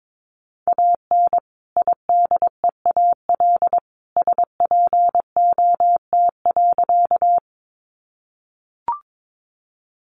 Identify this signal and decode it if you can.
Morse code, heard as a keyed tone, about 22 wpm, 700 Hz. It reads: AN IDEAL SPOT.
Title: Sentences from Top 1400 Words 22wpm